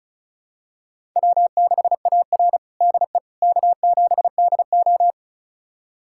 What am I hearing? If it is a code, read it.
Morse code, heard as a keyed tone, about 36 wpm, 690 Hz. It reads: W6AR DE K7DO
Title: USA Callsign DE Callsign - 4 Characters - 35wpm